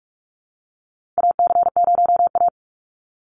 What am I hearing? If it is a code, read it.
Morse code, heard as a keyed tone, about 46 wpm, 700 Hz. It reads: AC0A
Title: NA SOTA Chaser Callsigns 45wpm